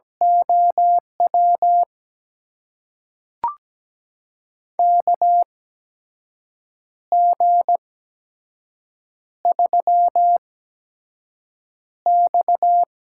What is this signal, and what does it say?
Morse code, heard as a keyed tone, about 17 wpm, spaced as for 6 wpm, 700 Hz. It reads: OW KG3X
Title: USA Call Signs - 8x Character Spacing - 17wpm